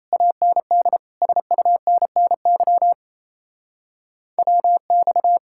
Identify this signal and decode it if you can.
Morse code, heard as a keyed tone, about 36 wpm, 700 Hz. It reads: AND SUNNY WX